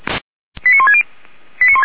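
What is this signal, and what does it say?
Five-tone selective call (ZVEI), heard as quick runs of five tones, tones of about 70 ms. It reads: 89160
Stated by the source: ZVEI five-tone selective call 89133, received by SDR, from the audio archive of Feuerwehr Orlamünde.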